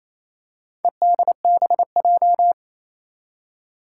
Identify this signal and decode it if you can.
Morse code, heard as a keyed tone, about 28 wpm, 700 Hz. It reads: EDBJ
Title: Four Challenging Letter-Number Combinations 28wpm